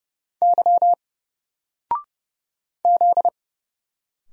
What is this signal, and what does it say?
Morse code, heard as a keyed tone, about 30 wpm, 700 Hz. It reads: Y Z
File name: Single-Letter-Number-30wpm